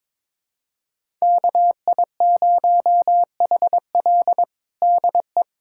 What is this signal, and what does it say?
Morse code, heard as a keyed tone, about 22 wpm, 700 Hz. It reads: KI0HL DE
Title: USA Callsign DE Callsign - 5 Characters - 22wpm